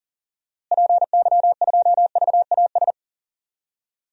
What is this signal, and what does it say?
Morse code, heard as a keyed tone, about 40 wpm, 700 Hz. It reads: PY2VAS
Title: DXCC-100 Call Signs - 6 Characters - Repeated 2x - 40wpm